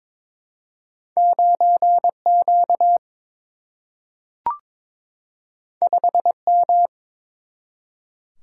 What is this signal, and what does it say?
Morse code, heard as a keyed tone, about 22 wpm, 700 Hz. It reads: 9Q 5M
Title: Single-Letter Single-Number 22wpm